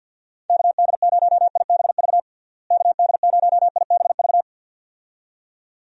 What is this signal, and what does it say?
Morse code, heard as a keyed tone, about 51 wpm, 700 Hz. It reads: KD0IBV KD0IBV
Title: USA Call Signs - 6 Characters - Repeated 2x - 50wpm